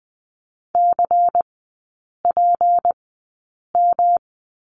Morse code, heard as a keyed tone, about 20 wpm, spaced as for 8 wpm, 700 Hz. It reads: CPM
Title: 152 Lesson 25e - Letter p - Three Characters 20wpm